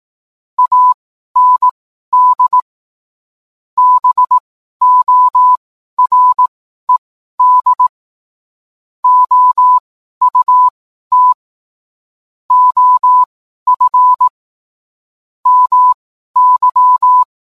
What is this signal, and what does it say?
Morse code, heard as a keyed tone, about 18 wpm, spaced as for 12 wpm, 1000 Hz. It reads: AND BORED OUT OF MY